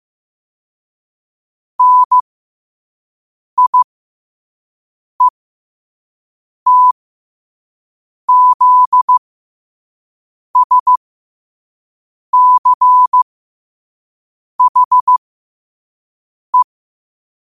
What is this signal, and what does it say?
Morse code, heard as a keyed tone, about 15 wpm, spaced as for 5 wpm, 1000 Hz. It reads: NIETZSCHE
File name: qotdfiles / 5wpm